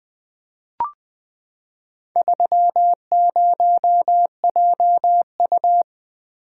Morse code, heard as a keyed tone, about 20 wpm, 700 Hz. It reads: 30JU
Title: Four Letter-Number Combinations - 20wpm